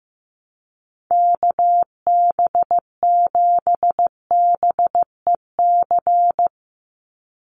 Morse code, heard as a keyed tone, about 15 wpm, 700 Hz. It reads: KB7BEC
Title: USA Call Signs 15wpm